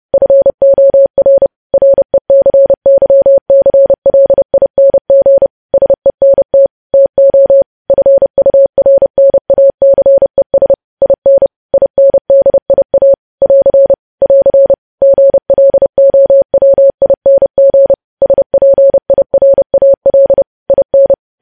Morse code, heard as a keyed tone, about 30 wpm, 550 Hz. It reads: FOR RECYCLING SENT TO FURNACES IN INDIA <AR> <AR> GLOWING SPIRAL IN
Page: news headlines in morse code at 30 wpm